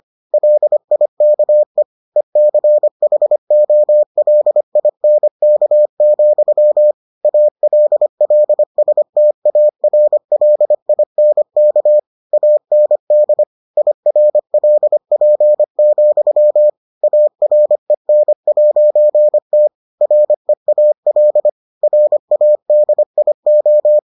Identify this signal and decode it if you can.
Morse code, heard as a keyed tone, about 25 wpm, 600 Hz. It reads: LIKE ECHOLINK, ALLSTARLINK AND IRLP, AREN'T REAL RADIO